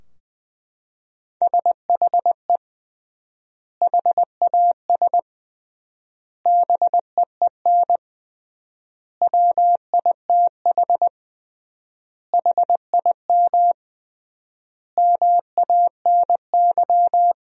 Morse code, heard as a keyed tone, about 20 wpm, 700 Hz. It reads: SHE HAS BEEN WITH HIM MANY
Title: 3x Word-Spacing - Sentences from Top 200 Words - 20wpm